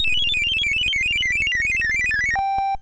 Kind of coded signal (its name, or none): none